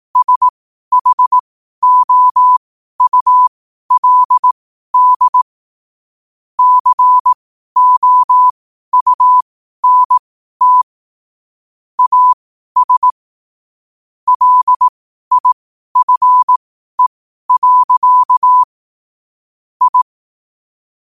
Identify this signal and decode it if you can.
Morse code, heard as a keyed tone, about 18 wpm, spaced as for 12 wpm, 1000 Hz. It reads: SHOULD COUNT AS LIFE. I
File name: qotdfiles / 12wpm